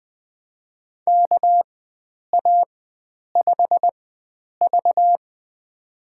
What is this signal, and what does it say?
Morse code, heard as a keyed tone, about 20 wpm, spaced as for 9 wpm, 700 Hz. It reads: KA5V
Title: USA Call Signs - 4 Characters - 4x Character Spacing - 20wpm